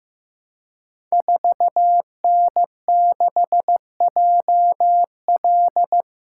Morse code, heard as a keyed tone, about 15 wpm, 700 Hz. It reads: 4N6JL
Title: Code Groups - 5 Letter:Numbers - 15wpm